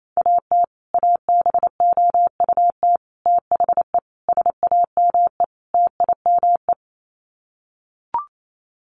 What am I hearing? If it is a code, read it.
Morse code, heard as a keyed tone, about 28 wpm, 690 Hz. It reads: AT ABOUT THE SAME TIME
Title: Sets of 5 Words Encore 28wpm